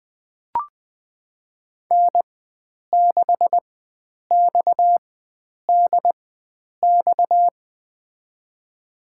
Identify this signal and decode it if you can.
Morse code, heard as a keyed tone, about 20 wpm, spaced as for 9 wpm, 700 Hz. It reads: N6XDX
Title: USA Call Signs - 5 Characters - 4x Character Spacing - 20wpm